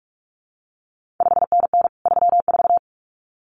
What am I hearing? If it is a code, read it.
Morse code, heard as a keyed tone, about 46 wpm, 700 Hz. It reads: 5NN 34
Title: CQ WW DX Contest Exchange Encore - 45wpm